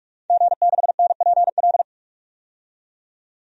Morse code, heard as a keyed tone, about 42 wpm, 700 Hz. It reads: G6NPL
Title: DXCC-100 Call Signs - 5 Characters - 45wpm